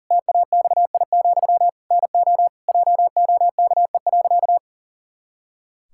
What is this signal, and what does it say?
Morse code, heard as a keyed tone, about 40 wpm, 700 Hz. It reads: TAXI, NO JOKE.
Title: Pangram Challenge v2 - 40wpm